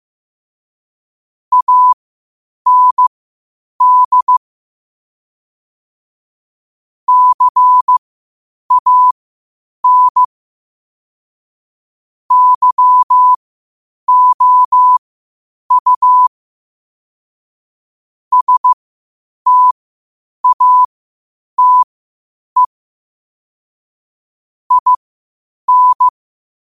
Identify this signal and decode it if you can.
Morse code, heard as a keyed tone, about 15 wpm, spaced as for 8 wpm, 1000 Hz. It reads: AND CAN YOU STATE IN
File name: qotdfiles / 8wpm